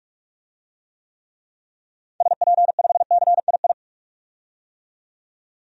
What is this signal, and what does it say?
Morse code, heard as a keyed tone, about 41 wpm, 700 Hz. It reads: SP5CII
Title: DXCC-100 Call Signs - 6 Characters - 45wpm